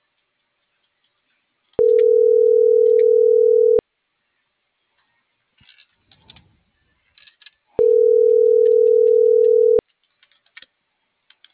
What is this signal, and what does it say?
Ringback tone (North America), 440 + 480 Hz.